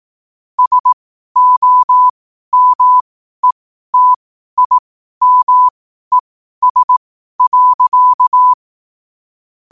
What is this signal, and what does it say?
Morse code, heard as a keyed tone, about 18 wpm, spaced as for 12 wpm, 1000 Hz. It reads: SOMETIMES.
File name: qotdfiles / 12wpm